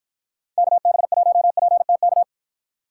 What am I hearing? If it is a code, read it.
Morse code, heard as a keyed tone, about 53 wpm, 700 Hz. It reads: KB1PTK